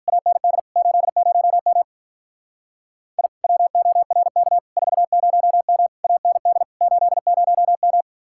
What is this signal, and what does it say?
Morse code, heard as a keyed tone, about 43 wpm, 700 Hz. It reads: AND 80M I WORK 40M AND 80M
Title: Speed-Racing Common QSO Phrases - 35wpm